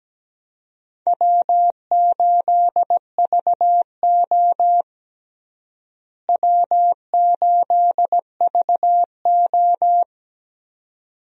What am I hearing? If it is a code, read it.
Morse code, heard as a keyed tone, about 17 wpm, 700 Hz. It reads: W8VO W8VO
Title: USA Call Signs - 4 Characters - Repeated 2x - 17wpm